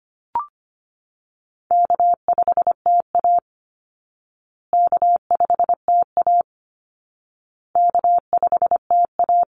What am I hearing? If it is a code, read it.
Morse code, heard as a keyed tone, about 25 wpm, 700 Hz. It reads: K5TA K5TA K5TA
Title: Call Signs Repeated 2x - 25wpm